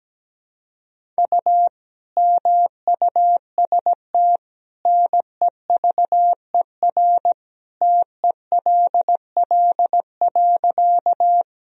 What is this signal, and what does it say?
Morse code, heard as a keyed tone, about 17 wpm, 700 Hz. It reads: U MUST NEVER TELL.
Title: Sentences from Top 300 Words and Abbreviations 17wpm